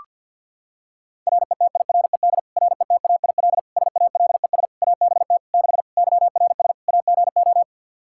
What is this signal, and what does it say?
Morse code, heard as a keyed tone, about 46 wpm, 700 Hz. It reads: RETIRED RETAIL SALES ABT 6 YRS AGO
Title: Practice QSO - Exchange 3 - More Info - 50wpm